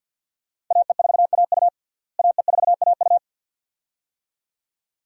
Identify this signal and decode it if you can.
Morse code, heard as a keyed tone, about 52 wpm, 700 Hz. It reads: AE4AU AE4AU
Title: USA Call Signs - 5 Characters - Repeated 2x - 50wpm